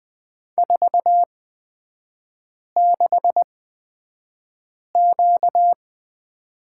Morse code, heard as a keyed tone, about 20 wpm, spaced as for 5 wpm, 700 Hz. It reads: 46Q
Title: 236 Lesson 38e - Letter q - Three Characters 20wpm